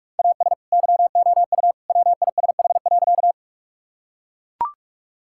Morse code, heard as a keyed tone, about 41 wpm, 700 Hz. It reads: AS YOU WISH.
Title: Sentences from Top 1600 Words 45wpm